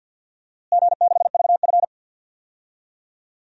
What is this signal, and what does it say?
Morse code, heard as a keyed tone, about 47 wpm, 700 Hz. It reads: G6VF
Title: DXCC-100 Call Signs - 4 Characters - Repeated 3x - 50wpm